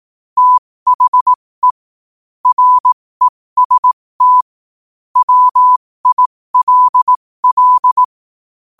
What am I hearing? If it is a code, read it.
Morse code, heard as a keyed tone, about 18 wpm, spaced as for 15 wpm, 1000 Hz. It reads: THE REST WILL
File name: qotdfiles / 15wpm